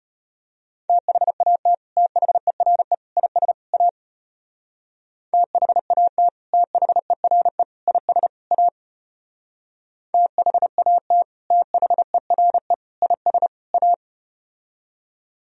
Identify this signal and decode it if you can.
Morse code, heard as a keyed tone, about 33 wpm, 700 Hz. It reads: THAT THERE IS A THAT THERE IS A THAT THERE IS A